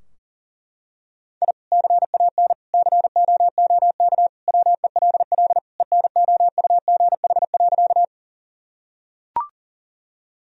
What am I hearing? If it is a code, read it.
Morse code, heard as a keyed tone, about 40 wpm, 700 Hz. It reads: I CAN COOK WELL ENOUGH.
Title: Sentences from Top 1000 Words - Review Part 4 - 40wpm